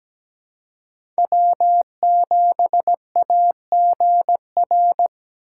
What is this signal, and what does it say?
Morse code, heard as a keyed tone, about 17 wpm, 700 Hz. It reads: W7AGR
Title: USA Call Signs - 5 Characters - 4x Character Spacing - 17wpm